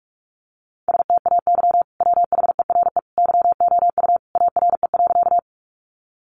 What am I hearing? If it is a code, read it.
Morse code, heard as a keyed tone, about 44 wpm, 700 Hz. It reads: STAY WHERE YOU ARE.
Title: Sentences from Top 500 Words 45wpm